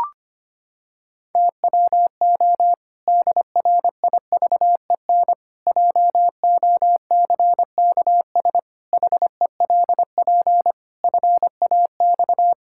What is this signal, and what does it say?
Morse code, heard as a keyed tone, about 25 wpm, 700 Hz. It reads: TWO DRIVEN JOCKS HELP FAX
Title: Pangram Challenge v2 - 25wpm